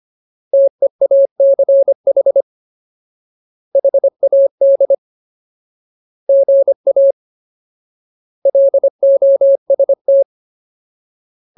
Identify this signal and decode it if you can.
Morse code, heard as a keyed tone, about 25 wpm, 550 Hz. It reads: TEACH HAD GA LOST
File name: Ultimate Word List / 25WPM Ultimate